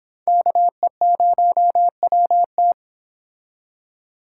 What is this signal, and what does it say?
Morse code, heard as a keyed tone, about 26 wpm, 700 Hz. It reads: KE0WT